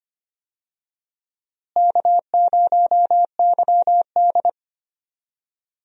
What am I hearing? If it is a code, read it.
Morse code, heard as a keyed tone, about 25 wpm, 700 Hz. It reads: K0YD
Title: USA Call Signs - 4 Characters - Repeated 3x - 25wpm